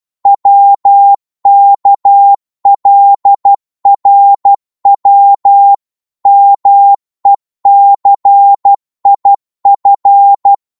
Morse code, heard as a keyed tone, about 12 wpm, 800 Hz. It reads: WKLRW MECIF
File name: Morse / ltr-12wpm